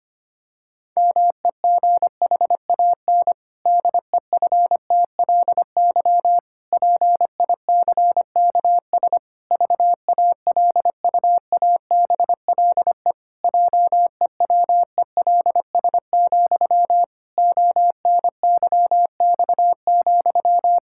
Morse code, heard as a keyed tone, about 25 wpm, 700 Hz. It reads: MEGHAN DEFTLY PICKS VALUABLE JEWELS, ONYX,
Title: Pangram Challenge v2 - 25wpm